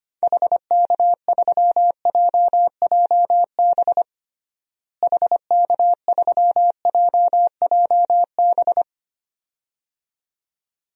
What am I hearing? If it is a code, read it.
Morse code, heard as a keyed tone, about 25 wpm, 700 Hz. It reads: HK3JJB HK3JJB
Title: DXCC-100 Call Signs - 6 Characters - Repeated 2x - 25wpm